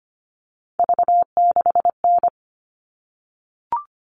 Morse code, heard as a keyed tone, about 25 wpm, 700 Hz. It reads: V6N
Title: All Cut Numbers - 3-Digit 25wpm